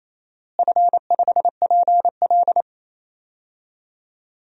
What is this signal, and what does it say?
Morse code, heard as a keyed tone, about 28 wpm, 700 Hz. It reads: F5PL
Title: DXCC-100 Call Signs - 4 Characters - Repeated 2x - 28wpm